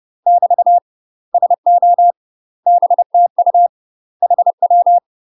Morse code, heard as a keyed tone, about 30 wpm, 700 Hz. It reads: <BT> SO BTU HW
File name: UK MASTER QSOs TRAINING / 30WPM / UK MASTER QSOs 0027